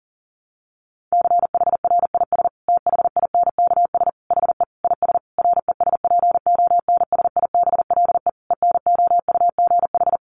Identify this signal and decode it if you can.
Morse code, heard as a keyed tone, about 37 wpm, 700 Hz. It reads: CHRIS THINKS HE IS RESPONSIBLE ENOUGH
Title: Easy Phrases 40wpm